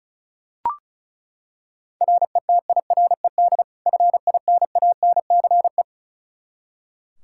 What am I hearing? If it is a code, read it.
Morse code, heard as a keyed tone, about 36 wpm, 700 Hz. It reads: RETIRED FINANCE
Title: Practice QSO - Exchange 3 - More Info - 35wpm